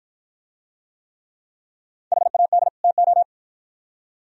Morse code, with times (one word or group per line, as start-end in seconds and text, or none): 2.1-2.7 HAD
2.8-3.2 TO